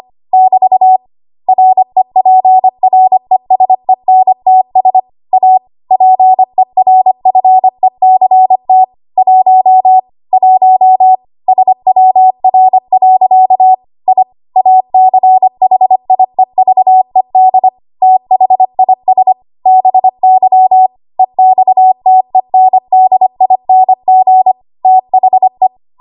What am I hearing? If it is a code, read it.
Morse code, heard as a keyed tone, about 25 wpm, 750 Hz. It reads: <BT> REPRESENTS A PERFECT 1 1 SWR. I ACHIEVED THIS BY EXTENDING THE
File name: Archive / 25 WPM / 25WPM